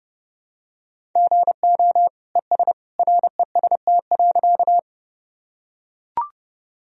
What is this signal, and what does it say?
Morse code, heard as a keyed tone, about 30 wpm, 700 Hz. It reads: GO ES REST.
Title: Sentences from Top 500 Words and Abbreviations 30wpm